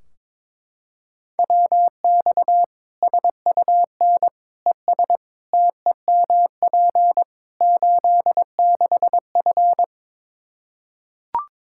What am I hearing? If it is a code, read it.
Morse code, heard as a keyed tone, about 22 wpm, 700 Hz. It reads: WX SUN ES TEMP 86F
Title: K1USN SST QSO Practice 22wpm